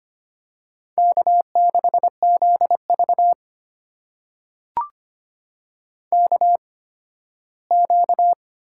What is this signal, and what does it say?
Morse code, heard as a keyed tone, about 25 wpm, spaced as for 15 wpm, 700 Hz. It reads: K6ZV KQ
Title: USA Call Signs - 4 Characters - 8x Character Spacing - 25wpm